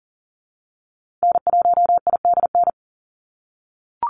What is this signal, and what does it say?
Morse code, heard as a keyed tone, about 42 wpm, 700 Hz. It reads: N1IDN
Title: NA SOTA Activator Callsigns 40wpm